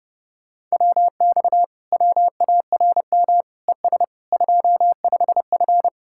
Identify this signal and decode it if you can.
Morse code, heard as a keyed tone, about 28 wpm, 700 Hz. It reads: WX WARM ES 25F
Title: Speed-Racing Common QSO Phrases - 30wpm